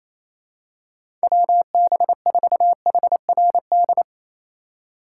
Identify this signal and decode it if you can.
Morse code, heard as a keyed tone, about 28 wpm, 700 Hz. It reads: WB4HRD
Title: USA Call Signs - 2x Character Spacing - 28wpm